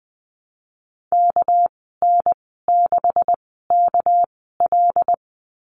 Morse code, heard as a keyed tone, about 20 wpm, spaced as for 14 wpm, 700 Hz. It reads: KN6KL
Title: 213 Lesson 34g - Number 8 - Call Signs 20wpm